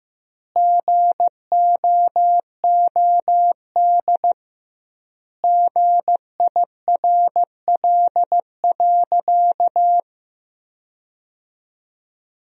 Morse code, heard as a keyed tone, about 16 wpm, 700 Hz. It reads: GOOD GIRL.
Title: 2x Word-Spacing - Sentences from Top 400 Words - 15wpm